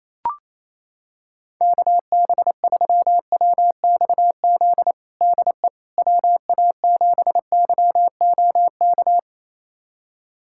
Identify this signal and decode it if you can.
Morse code, heard as a keyed tone, about 27 wpm, 700 Hz. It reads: KB3WXZ DE WA7YOK
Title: USA Callsign DE Callsign - 6 Characters - 28wpm